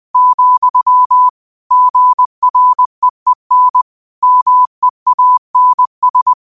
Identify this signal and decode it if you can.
Morse code, heard as a keyed tone, about 20 wpm, 1000 Hz. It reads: , GREEN MEANS
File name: qotdfiles / 20wpm